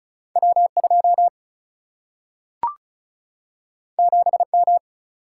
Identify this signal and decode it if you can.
Morse code, heard as a keyed tone, about 36 wpm, 700 Hz. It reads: W2 7M
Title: Single-Letter Single-Number 35wpm